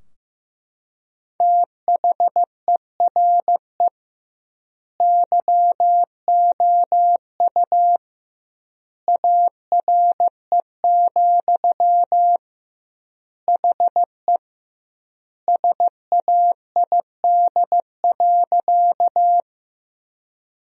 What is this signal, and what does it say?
Morse code, heard as a keyed tone, about 16 wpm, 700 Hz. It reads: THERE YOU ARE, HE SAID.
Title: 2x Word-Spacing - Sentences from Top 100 Words - 15wpm